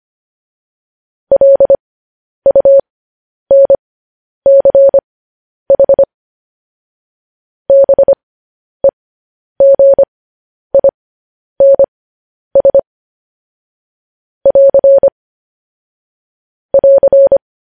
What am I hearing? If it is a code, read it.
Morse code, heard as a keyed tone, about 25 wpm, spaced as for 10 wpm, 550 Hz. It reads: LUNCH BEGINS <AR> <AR>
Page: news headlines in morse code at 10 wpm